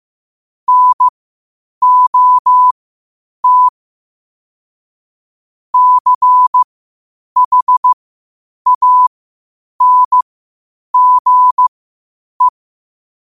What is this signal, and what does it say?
Morse code, heard as a keyed tone, about 15 wpm, spaced as for 8 wpm, 1000 Hz. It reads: NOT CHANGE
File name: qotdfiles / 8wpm